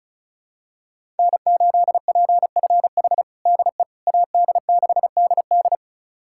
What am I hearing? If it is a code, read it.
Morse code, heard as a keyed tone, about 36 wpm, 700 Hz. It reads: N8PFH DE AD6DD
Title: USA Callsign DE Callsign - 5 Characters - 35wpm